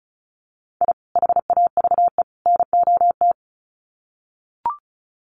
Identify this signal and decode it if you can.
Morse code, heard as a keyed tone, about 34 wpm, 700 Hz. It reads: I HAVE NOT
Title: Sets of 3 Words 35wpm